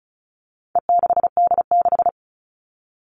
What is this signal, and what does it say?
Morse code, heard as a keyed tone, about 34 wpm, 700 Hz. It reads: E6D6
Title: All Cut Numbers - 4-Digit 35wpm